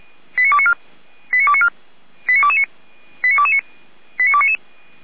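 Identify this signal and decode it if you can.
Five-tone selective call (ZVEI), heard as quick runs of five tones, tones of about 80 ms. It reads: 89294 89294 89229 89229 89299
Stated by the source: ZVEI five-tone selective call received with an SDR, from the Feuerwehr Orlamünde fire brigade server.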